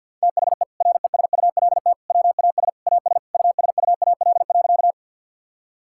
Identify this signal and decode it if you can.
Morse code, heard as a keyed tone, about 46 wpm, 700 Hz. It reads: THE RESULT WAS AS USUAL.
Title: Sentences from Top 1000 Words - Review Part 5 - 50wpm